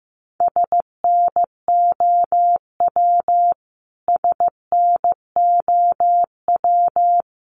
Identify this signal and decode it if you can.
Morse code, heard as a keyed tone, about 15 wpm, 700 Hz. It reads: SNOW SNOW
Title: Top 500 Words Repeated 15wpm